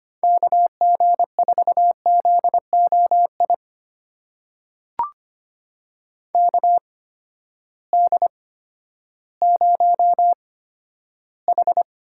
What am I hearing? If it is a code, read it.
Morse code, heard as a keyed tone, about 25 wpm, spaced as for 12 wpm, 700 Hz. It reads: KG4ZOI KD0H
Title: USA Call Signs - 6 Characters - 8x Character Spacing - 25wpm